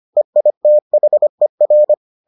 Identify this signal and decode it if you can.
Morse code, heard as a keyed tone, about 25 wpm, 600 Hz. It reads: EITHER